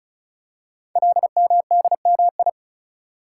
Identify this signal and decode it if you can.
Morse code, heard as a keyed tone, about 36 wpm, 700 Hz. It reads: LMDMI
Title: Code Groups - 5 Letters - 35wpm